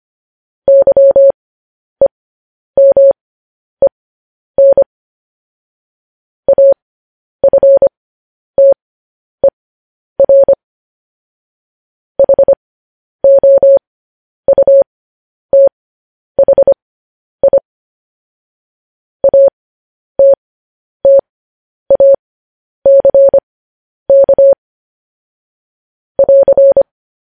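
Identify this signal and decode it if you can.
Morse code, heard as a keyed tone, about 25 wpm, spaced as for 10 wpm, 550 Hz. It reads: YEMEN AFTER HOUTHI ATTACK <AR>
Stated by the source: Morse code news headlines.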